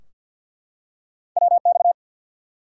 Morse code, heard as a keyed tone, about 48 wpm, 700 Hz. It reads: WX
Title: Top 90 QSO Elements 50wpm